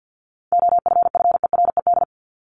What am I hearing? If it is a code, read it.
Morse code, heard as a keyed tone, about 50 wpm, 700 Hz. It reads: OFFERED
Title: ED Suffix - Top 100 Words with Base Words 50wpm